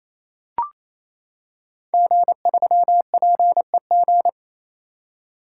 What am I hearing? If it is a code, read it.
Morse code, heard as a keyed tone, about 28 wpm, 700 Hz. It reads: G3PEG
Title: Code Groups - 5 Letter:Numbers - 28wpm